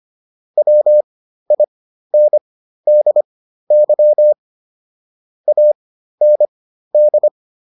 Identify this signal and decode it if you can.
Morse code, heard as a keyed tone, about 25 wpm, spaced as for 13 wpm, 600 Hz. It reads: WINDY AND